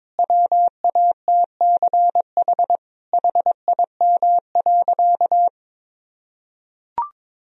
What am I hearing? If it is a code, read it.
Morse code, heard as a keyed tone, about 22 wpm, 700 Hz. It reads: WATCH HIM.
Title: Sentences from Top 1000 Words - Review Part 1 - 22wpm